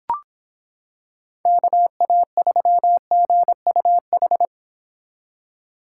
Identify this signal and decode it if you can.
Morse code, heard as a keyed tone, about 26 wpm, 700 Hz. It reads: KA3GUH